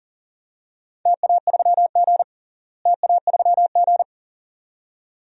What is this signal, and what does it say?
Morse code, heard as a keyed tone, about 40 wpm, 700 Hz. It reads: TA3G TA3G
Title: DXCC-100 Call Signs - 4 Characters - Repeated 2x - 40wpm